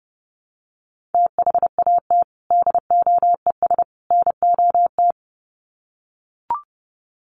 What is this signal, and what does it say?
Morse code, heard as a keyed tone, about 30 wpm, 700 Hz. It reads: THAT DOES NOT
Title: Sets of 3 Words 30wpm